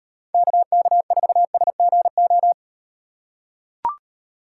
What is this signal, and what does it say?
Morse code, heard as a keyed tone, about 37 wpm, 700 Hz. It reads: KK4SGO